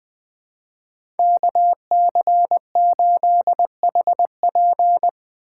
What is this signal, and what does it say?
Morse code, heard as a keyed tone, about 20 wpm, 700 Hz. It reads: KC8HP
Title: USA Call Signs - 5 Characters - 4x Character Spacing - 20wpm